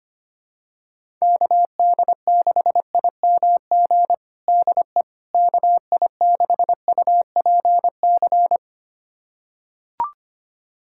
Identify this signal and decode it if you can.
Morse code, heard as a keyed tone, about 25 wpm, 700 Hz. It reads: KD6IMG DE KI6UPC
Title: USA Callsign DE Callsign 25wpm